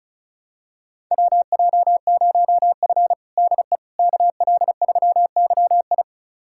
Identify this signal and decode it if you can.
Morse code, heard as a keyed tone, about 36 wpm, 700 Hz. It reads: WJ0F DE KL3YI
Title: USA Callsign DE Callsign 35wpm